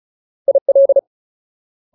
Morse code, heard as a keyed tone, about 35 wpm, 550 Hz. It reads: IL
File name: Ultimate Word List / 35WPM Ultimate